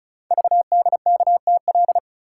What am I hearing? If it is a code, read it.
Morse code, heard as a keyed tone, about 36 wpm, 690 Hz. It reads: VDKTL
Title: Code Groups - 5 Letters - 35wpm